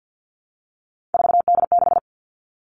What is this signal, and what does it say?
Morse code, heard as a keyed tone, about 50 wpm, 700 Hz. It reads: 4D6
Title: All Cut Numbers - 3-Digit 50wpm